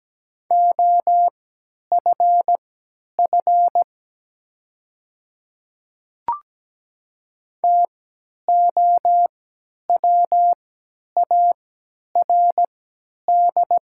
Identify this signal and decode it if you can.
Morse code, heard as a keyed tone, about 17 wpm, spaced as for 10 wpm, 700 Hz. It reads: OFF TOWARD
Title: 3x Character Spacing - Top 500 Words - 17wpm